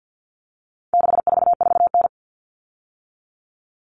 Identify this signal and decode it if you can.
Morse code, heard as a keyed tone, about 50 wpm, 700 Hz. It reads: 644N
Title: All Cut Numbers - 4-Digit 50wpm